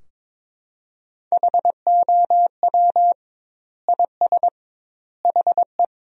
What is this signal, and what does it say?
Morse code, heard as a keyed tone, about 22 wpm, 700 Hz. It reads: HOW IS HE
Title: 2x Word-Spacing - Sentences from Top 100 Words - 22wpm